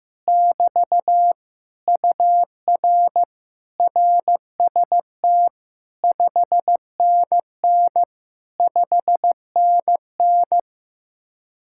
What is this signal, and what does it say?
Morse code, heard as a keyed tone, about 15 wpm, 700 Hz. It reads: <BT> UR RST 5NN 5NN
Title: Speed-Racing Common QSO Phrases Encore - 15wpm